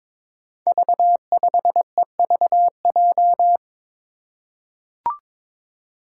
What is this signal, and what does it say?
Morse code, heard as a keyed tone, about 22 wpm, 700 Hz. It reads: V5EVJ
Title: Five Challenging Letter-Number Combinations 22wpm